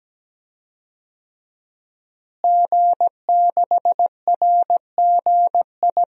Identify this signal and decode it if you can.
Morse code, heard as a keyed tone, about 17 wpm, 700 Hz. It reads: G6RGI
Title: DXCC-100 Call Signs - 5 Characters - Repeated 2x - 17wpm